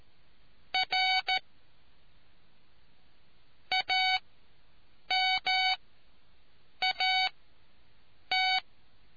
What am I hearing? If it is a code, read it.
Morse code, heard as a keyed tone, about 14 wpm, spaced as for 7 wpm, 750 Hz. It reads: R AMAT